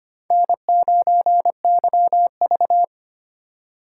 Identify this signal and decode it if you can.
Morse code, heard as a keyed tone, about 25 wpm, 700 Hz. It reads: N9YV
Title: USA Call Signs - 4 Characters - 4x Character Spacing - 25wpm